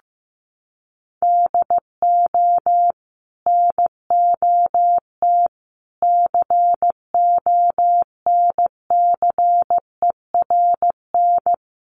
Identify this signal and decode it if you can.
Morse code, heard as a keyed tone, about 15 wpm, 700 Hz. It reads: DO NOT CONCERN
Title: Sentences from Top 1000 Words 15wpm